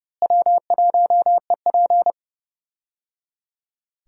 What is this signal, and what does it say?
Morse code, heard as a keyed tone, about 30 wpm, 700 Hz. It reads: W1EP
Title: USA Call Signs - 4 Characters - Repeated 3x - 30wpm